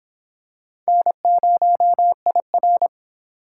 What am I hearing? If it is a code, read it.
Morse code, heard as a keyed tone, about 23 wpm, 700 Hz. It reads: N0IR